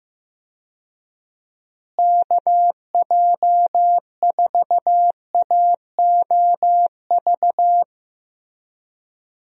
Morse code, heard as a keyed tone, about 15 wpm, 700 Hz. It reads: KJ4AOV
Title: USA Call Signs - 6 Characters - Repeated 2x - 15wpm